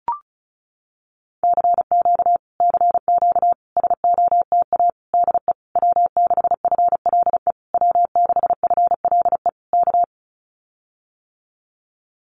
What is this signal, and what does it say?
Morse code, heard as a keyed tone, about 34 wpm, 700 Hz. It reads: CQ CQ SOTA DE W6FLE W6FLE K
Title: Common QSO Phrases Encore 35wpm